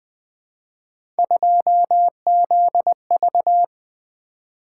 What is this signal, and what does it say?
Morse code, heard as a keyed tone, about 20 wpm, 700 Hz. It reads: 2ZV
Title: Three Letter-Number Combinations - 20wpm